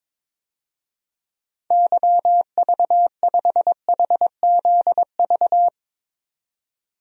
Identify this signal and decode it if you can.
Morse code, heard as a keyed tone, about 22 wpm, 700 Hz. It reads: YV5HZV
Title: DXCC-100 Call Signs - 6 Characters - 22wpm